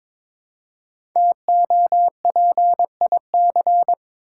Morse code, heard as a keyed tone, about 22 wpm, 700 Hz. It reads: TOPIC
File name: 5 Letter Words - First 3 Letters Given - 22wpm